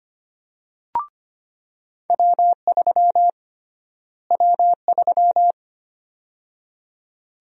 Morse code, heard as a keyed tone, about 25 wpm, 700 Hz. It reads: W3 W3
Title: USA Call Sign Prefixes - Repeated 2x - 25wpm